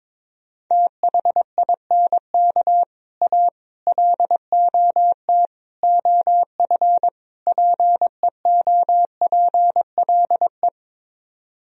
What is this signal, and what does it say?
Morse code, heard as a keyed tone, about 22 wpm, 700 Hz. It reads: THINK A LOT OF PEOPLE
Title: Speed-Racing - Sets of 5 Words 22wpm